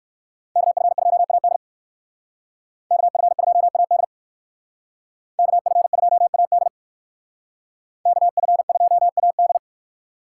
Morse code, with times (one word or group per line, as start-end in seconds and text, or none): 0.6-1.6 KF2AD
2.9-4.0 KF2AD
5.4-6.7 KF2AD
8.1-9.6 KF2AD